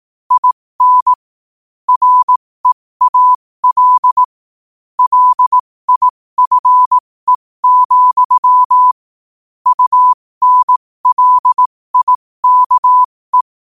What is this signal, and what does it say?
Morse code, heard as a keyed tone, about 18 wpm, spaced as for 15 wpm, 1000 Hz. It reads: IN REAL LIFE, UNLIKE
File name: qotdfiles / 15wpm